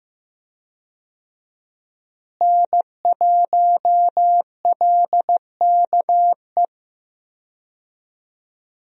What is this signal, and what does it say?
Morse code, heard as a keyed tone, about 15 wpm, 700 Hz. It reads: N1LKE
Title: USA Call Signs - 5 Characters - Repeated 2x - 15wpm